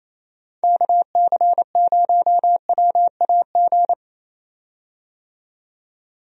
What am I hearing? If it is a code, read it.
Morse code, heard as a keyed tone, about 27 wpm, 700 Hz. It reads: KC0WAG
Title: USA Call Signs - 6 Characters - 28wpm